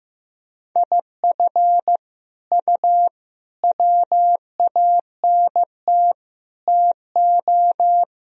Morse code, heard as a keyed tone, about 15 wpm, 700 Hz. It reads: IF U WANT TO
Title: Sentences from Top 200 Words and Abbreviations 15wpm